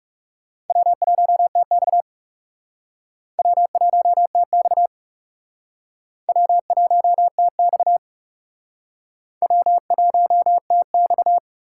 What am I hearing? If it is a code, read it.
Morse code, heard as a keyed tone, about 35 wpm, 700 Hz. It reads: W1TX W1TX W1TX W1TX
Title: Speed Racing - USA 1x2 Call Signs 30wpm